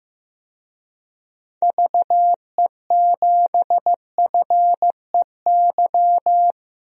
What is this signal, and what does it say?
Morse code, heard as a keyed tone, about 15 wpm, 700 Hz. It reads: VE7FEY
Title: DXCC-100 Call Signs - 6 Characters - Repeated 3x - 15wpm